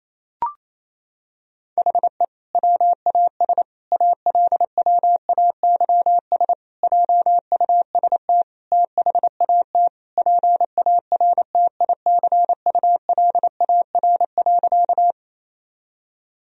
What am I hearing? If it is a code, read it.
Morse code, heard as a keyed tone, about 28 wpm, 700 Hz. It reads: HE WAS ALWAYS JUST THAT PARTICULAR.
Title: Sentences from Top 1000 Words - Review Part 4 - 28wpm